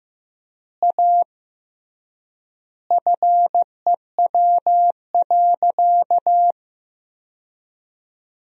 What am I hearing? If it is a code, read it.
Morse code, heard as a keyed tone, about 16 wpm, 700 Hz. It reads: A FEW.
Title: 3x Word-Spacing - Sentences from Top 300 Words - 15wpm